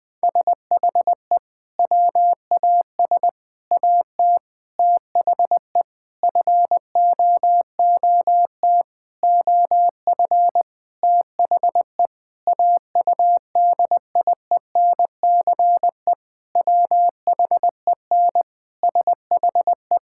Morse code, heard as a keyed tone, about 20 wpm, 700 Hz. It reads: SHE WAS AT THE FOOT OF THE AUDIENCE WHEN SHE
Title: Sentences from Top 1000 Words - Review Part 5 - 20wpm